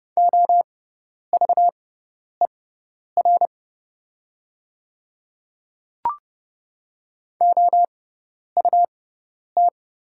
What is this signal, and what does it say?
Morse code, heard as a keyed tone, about 30 wpm, spaced as for 10 wpm, 690 Hz. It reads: OVER OUT
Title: 6x Character Spacing - Top 300 Words - 30wpm